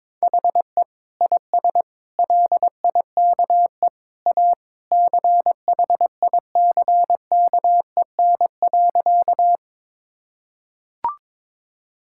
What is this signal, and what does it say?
Morse code, heard as a keyed tone, about 21 wpm, 700 Hz. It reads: HE IS LIKE A CHICKEN.
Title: Sentences from Top 1000 Words - Review Part 5 - 22wpm